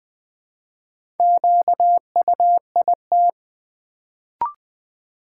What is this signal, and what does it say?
Morse code, heard as a keyed tone, about 20 wpm, 700 Hz. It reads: QUIT
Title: Build A Word - 1 Letter At A Time - 4-Letter Words - 20wpm